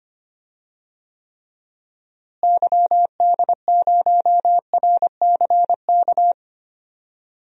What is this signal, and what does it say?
Morse code, heard as a keyed tone, about 25 wpm, 700 Hz. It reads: YD0RCK
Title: DXCC-100 Call Signs - 6 Characters - Repeated 3x - 25wpm